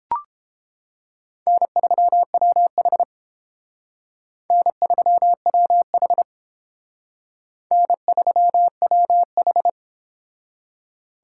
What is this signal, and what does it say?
Morse code, heard as a keyed tone, about 30 wpm, 700 Hz. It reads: N3WH N3WH N3WH